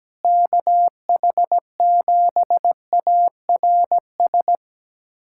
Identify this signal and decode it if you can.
Morse code, heard as a keyed tone, about 17 wpm, 700 Hz. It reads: KH7ARS
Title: USA Callsign DE Callsign - 6 Characters - 17wpm